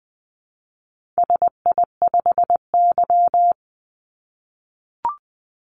Morse code, heard as a keyed tone, about 20 wpm, 700 Hz. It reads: SI5Y
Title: Call Signs - 20wpm